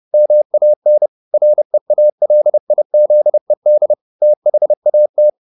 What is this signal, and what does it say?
Morse code, heard as a keyed tone, about 30 wpm, 600 Hz. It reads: MAN REALIZED THAT
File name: the old man and the sea / the old man and the sea 1 30wpm